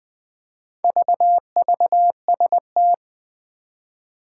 Morse code, heard as a keyed tone, about 20 wpm, 700 Hz. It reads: VVST
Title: Four Challenging Letter-Number Combinations 20wpm